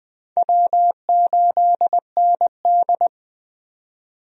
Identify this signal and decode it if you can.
Morse code, heard as a keyed tone, about 20 wpm, 700 Hz. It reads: W8ND
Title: USA Call Signs - 4 Characters - 20wpm